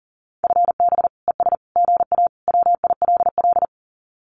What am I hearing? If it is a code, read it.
Morse code, heard as a keyed tone, about 37 wpm, 700 Hz. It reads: FB ES GA WILL
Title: Common QSO Phrases Encore 40wpm